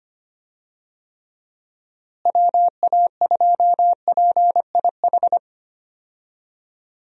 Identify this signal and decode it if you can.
Morse code, heard as a keyed tone, about 25 wpm, 700 Hz. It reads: WA2PIH
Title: USA Call Signs - 6 Characters - Repeated 3x - 25wpm